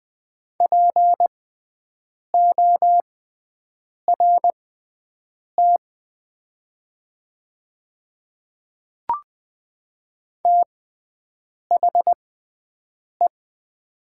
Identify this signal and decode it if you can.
Morse code, heard as a keyed tone, about 20 wpm, spaced as for 7 wpm, 700 Hz. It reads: PORT THE
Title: 6x Character Spacing - Top 200 Words - 20wpm